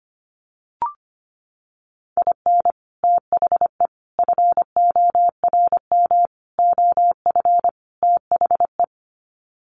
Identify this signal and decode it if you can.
Morse code, heard as a keyed tone, about 25 wpm, 700 Hz. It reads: IN THE FORM OF THE
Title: Sets of 5 Words Encore2 25wpm